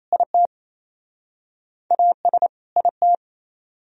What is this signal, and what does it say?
Morse code, heard as a keyed tone, about 38 wpm, 700 Hz. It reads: IT AS IT